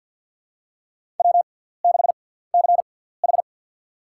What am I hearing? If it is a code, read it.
Morse code, heard as a keyed tone, about 47 wpm, spaced as for 17 wpm, 690 Hz. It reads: W6CH
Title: USA Call Signs - 4 Characters - 6x Character Spacing - 50wpm